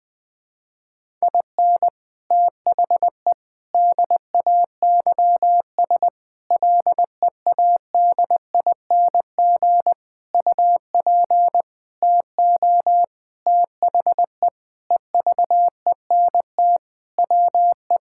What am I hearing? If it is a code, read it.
Morse code, heard as a keyed tone, about 21 wpm, 700 Hz. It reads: IN THE DAYS LEADING UP TO THE EVENT WE
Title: Sentences with 6-Word Phrases - 20wpm